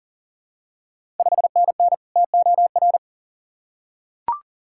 Morse code, heard as a keyed tone, about 40 wpm, 700 Hz. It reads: 5NN TOR
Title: Ontario QSO Party Exchange 40wpm